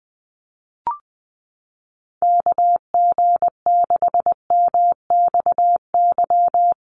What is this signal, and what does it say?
Morse code, heard as a keyed tone, about 20 wpm, 700 Hz. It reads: KG6MXY
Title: USA Call Signs Repeated 3x 20wpm